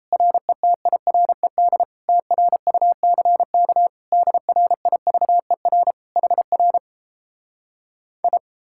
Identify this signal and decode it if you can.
Morse code, heard as a keyed tone, about 37 wpm, 700 Hz. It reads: RETIRED TRUCK DRIVER HR I